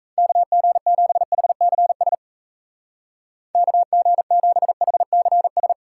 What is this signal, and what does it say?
Morse code, heard as a keyed tone, about 37 wpm, 700 Hz. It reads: KG7HCS KG7HCS